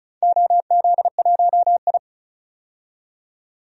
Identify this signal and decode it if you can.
Morse code, heard as a keyed tone, about 36 wpm, 690 Hz. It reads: OZ1I
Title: DXCC-100 Call Signs - 4 Characters - 35wpm